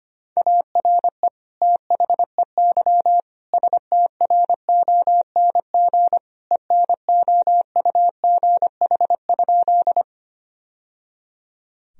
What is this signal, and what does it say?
Morse code, heard as a keyed tone, about 25 wpm, 700 Hz. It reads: ARE THEY STRONG ENOUGH?
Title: Sentences from Top 1000 Words - Review Part 3 - 25wpm